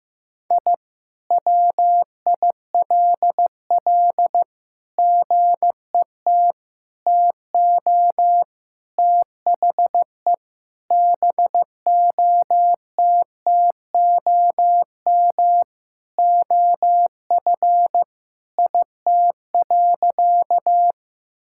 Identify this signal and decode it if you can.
Morse code, heard as a keyed tone, about 15 wpm, 700 Hz. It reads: I WILL GET TO THE BOTTOM OF IT.
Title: Sentences from Top 1000 Words - Review Part 6 - 15wpm